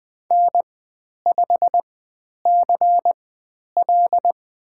Morse code, heard as a keyed tone, about 20 wpm, spaced as for 10 wpm, 700 Hz. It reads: N5CL
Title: 095 Lesson 16g - Letter c - Call Signs 20wpm